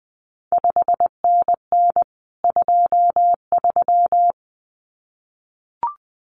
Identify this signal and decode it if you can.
Morse code, heard as a keyed tone, about 20 wpm, 700 Hz. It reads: 5NN 23
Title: CQ WW DX Contest Exchange Encore - 20wpm